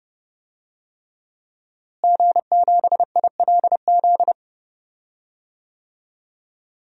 Morse code, heard as a keyed tone, about 32 wpm, 700 Hz. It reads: G7ILZ
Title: DXCC-100 Call Signs - 5 Characters - Repeated 3x - 30wpm